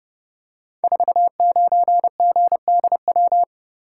Morse code, heard as a keyed tone, about 31 wpm, 700 Hz. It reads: 49GDW
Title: Code Groups - 5 Letter:Numbers - 30wpm